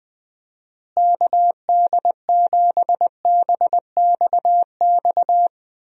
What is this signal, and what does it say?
Morse code, heard as a keyed tone, about 21 wpm, 700 Hz. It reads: KD7BXX
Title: USA Call Signs - 6 Characters - Repeated 3x - 20wpm